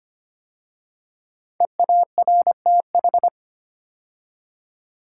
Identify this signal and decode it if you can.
Morse code, heard as a keyed tone, about 25 wpm, 700 Hz. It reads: EARTH
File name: 5 Letter Words - First 3 Letters Given - 25wpm